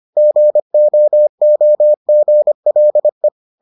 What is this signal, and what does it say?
Morse code, heard as a keyed tone, about 25 wpm, 600 Hz. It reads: GOOGLE